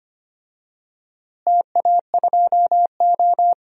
Morse code, heard as a keyed tone, about 25 wpm, 700 Hz. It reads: TA2O
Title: DXCC-100 Call Signs - 4 Characters - Repeated 3x - 25wpm